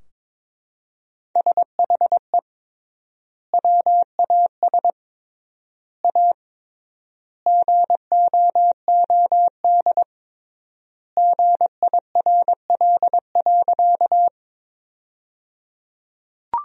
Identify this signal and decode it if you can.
Morse code, heard as a keyed tone, about 22 wpm, 700 Hz. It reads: SHE WAS A GOOD GIRL.
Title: 3x Word-Spacing - Sentences from Top 400 Words - 22wpm